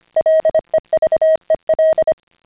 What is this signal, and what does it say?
Morse code, heard as a keyed tone, about 25 wpm, 620 Hz. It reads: LEVEL
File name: CW-pile up